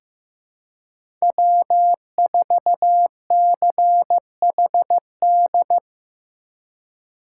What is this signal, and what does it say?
Morse code, heard as a keyed tone, about 15 wpm, 700 Hz. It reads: W4CHD
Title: USA Call Signs - 5 Characters - 4x Character Spacing - 15wpm